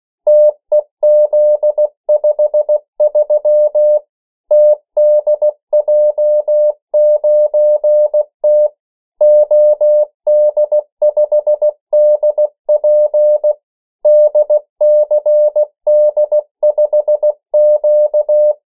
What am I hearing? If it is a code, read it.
Morse code, heard as a keyed tone, about 16 wpm, 600 Hz. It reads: TEZ53 TDJ9T OD5DP DCD5Q
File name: morse mp3 / 14 urok 13